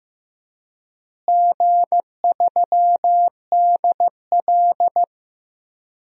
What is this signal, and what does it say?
Morse code, heard as a keyed tone, about 16 wpm, 700 Hz. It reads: G3DL
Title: DXCC-100 Call Signs - 4 Characters - Repeated 3x - 15wpm